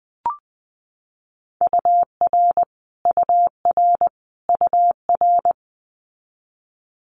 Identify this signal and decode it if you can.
Morse code, heard as a keyed tone, about 20 wpm, 700 Hz. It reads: UR UR UR
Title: Multi-letter Phonemes Repeated 3x 20wpm